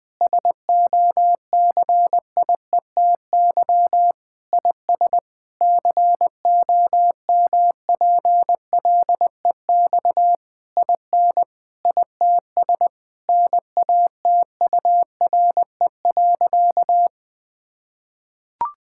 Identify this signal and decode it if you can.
Morse code, heard as a keyed tone, about 21 wpm, 700 Hz. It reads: SOCIETY IS COMPLEX IN ITS NATURE.
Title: Sentences from Top 1000 Words - Review Part 6 - 20wpm